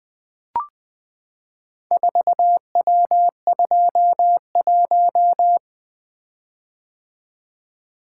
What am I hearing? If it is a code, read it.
Morse code, heard as a keyed tone, about 20 wpm, 700 Hz. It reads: 4W21
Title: Four Letter-Number Combinations - 20wpm